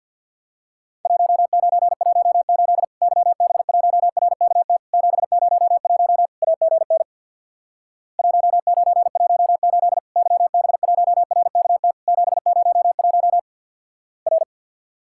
Morse code, heard as a keyed tone, about 50 wpm, 700 Hz. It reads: 1918 YB1RKT 701 AGN 1918 YB1RKT 701 R
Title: Worked All Europe DX Contest - QTC Exchange with Repeats - 50wpm